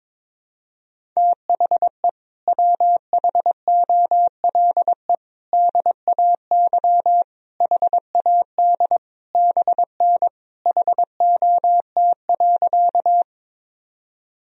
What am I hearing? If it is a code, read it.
Morse code, heard as a keyed tone, about 21 wpm, 700 Hz. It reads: THE WHOLE DAY HAD BN HOT.
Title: Sentences from Top 500 Words and Abbreviations 22wpm